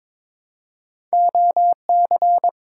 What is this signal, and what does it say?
Morse code, heard as a keyed tone, about 22 wpm, 700 Hz. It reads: OC
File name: practice / 22wpm / Two Letter-Number Combinations - 22wpm